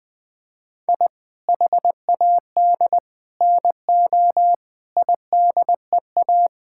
Morse code, heard as a keyed tone, about 21 wpm, 700 Hz. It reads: I HAD NO IDEA
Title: Sets of 8 Words - 20wpm